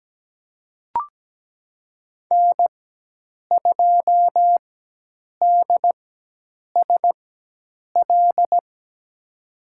Morse code, heard as a keyed tone, about 17 wpm, spaced as for 8 wpm, 700 Hz. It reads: N2DSL
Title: USA Call Signs - 5 Characters - 4x Character Spacing - 17wpm